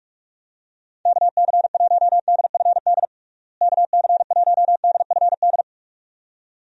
Morse code, heard as a keyed tone, about 46 wpm, 700 Hz. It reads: KC1DFD KC1DFD
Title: USA Call Signs - 6 Characters - Repeated 2x - 45wpm